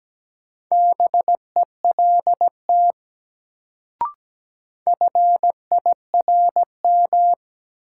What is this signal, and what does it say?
Morse code, heard as a keyed tone, about 17 wpm, 700 Hz. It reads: BELT FIRM
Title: Sentences from Top 1400 Words - New Word Drill - 17wpm